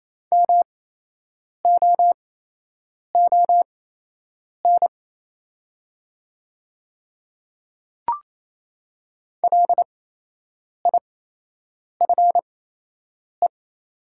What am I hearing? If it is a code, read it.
Morse code, heard as a keyed tone, about 28 wpm, spaced as for 8 wpm, 700 Hz. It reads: MOON LIFE
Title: 8x Character Spacing - Top 500 Words - 28wpm